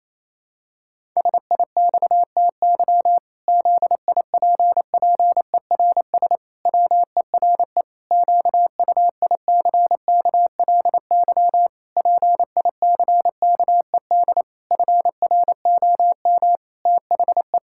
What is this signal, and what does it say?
Morse code, heard as a keyed tone, about 28 wpm, 700 Hz. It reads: SIXTY ZIPPERS WERE QUICKLY PICKED FROM THE
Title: Pangram Challenge v2 - 28wpm